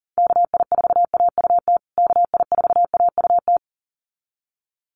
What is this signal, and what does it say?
Morse code, heard as a keyed tone, about 40 wpm, 700 Hz. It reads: KI4AUT KI4AUT
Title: USA Call Signs Repeated 2x 40wpm